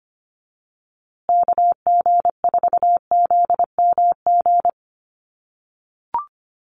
Morse code, heard as a keyed tone, about 25 wpm, 700 Hz. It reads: KG4ZMG
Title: USA Call Signs Repeated 2x 25wpm